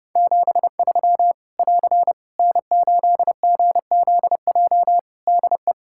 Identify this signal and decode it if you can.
Morse code, heard as a keyed tone, about 28 wpm, 700 Hz. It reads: 73 <AR> N8GZJ DE
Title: Practice QSO - Exchange 4 - Ending - 30wpm